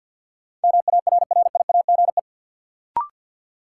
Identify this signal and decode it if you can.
Morse code, heard as a keyed tone, about 53 wpm, 700 Hz. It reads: MARRIAGE
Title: Sentences from Top 800 Words - New Word Drill - 50wpm